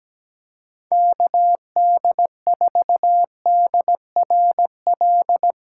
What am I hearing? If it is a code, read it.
Morse code, heard as a keyed tone, about 17 wpm, 700 Hz. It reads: KD4DRL
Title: USA Callsign DE Callsign - 6 Characters - 17wpm